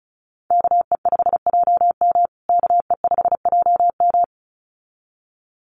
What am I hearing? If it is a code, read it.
Morse code, heard as a keyed tone, about 34 wpm, 700 Hz. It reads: KE5JM KE5JM
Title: USA Call Signs Repeated 2x 35wpm